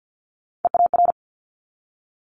Morse code, heard as a keyed tone, about 50 wpm, 700 Hz. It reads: EAR
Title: Multi-letter Phonemes 50wpm